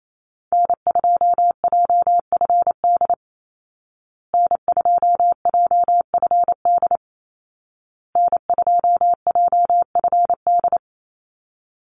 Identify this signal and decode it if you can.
Morse code, heard as a keyed tone, about 28 wpm, 700 Hz. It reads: N2JFD N2JFD N2JFD
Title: Call Signs Repeated 2x - 28wpm